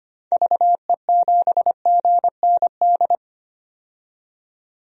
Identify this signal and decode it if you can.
Morse code, heard as a keyed tone, about 25 wpm, 700 Hz. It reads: VE7GND
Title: DXCC-100 Call Signs - 6 Characters - 25wpm